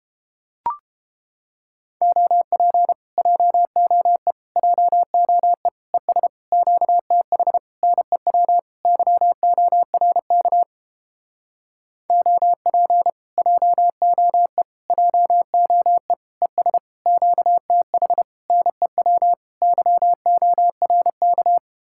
Morse code, heard as a keyed tone, about 29 wpm, 700 Hz. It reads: OP JOE JOE ES QTH NEW YORK OP JOE JOE ES QTH NEW YORK